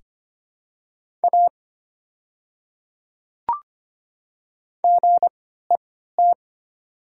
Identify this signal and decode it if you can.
Morse code, heard as a keyed tone, about 25 wpm, spaced as for 14 wpm, 700 Hz. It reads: A GET
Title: 3x Character Spacing - Top 200 Words - 25wpm